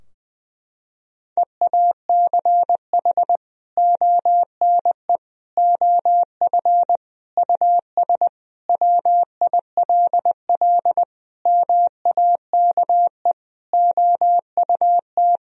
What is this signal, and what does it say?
Morse code, heard as a keyed tone, about 20 wpm, 700 Hz. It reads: EACH ONE OF US WILL MAKE OUT
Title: Sentences from Top 1000 Words - Review Part 2 - 20wpm